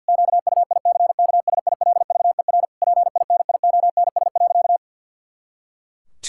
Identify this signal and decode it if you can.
Morse code, heard as a keyed tone, about 51 wpm, 700 Hz. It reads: QUICKSILVER PINIONS.
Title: Pangram Challenge v2 - 50wpm